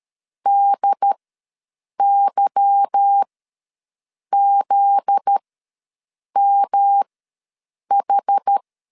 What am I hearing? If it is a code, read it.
Morse code, heard as a keyed tone, about 13 wpm, spaced as for 7 wpm, 800 Hz. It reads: DYZMH
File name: LEZIONI / 10 SIGNS / 10D COMA ws 12wpm 3618